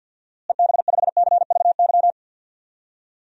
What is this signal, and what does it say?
Morse code, heard as a keyed tone, about 52 wpm, 700 Hz. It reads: EB5CVY
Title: DXCC-100 Call Signs - 6 Characters - Repeated 2x - 50wpm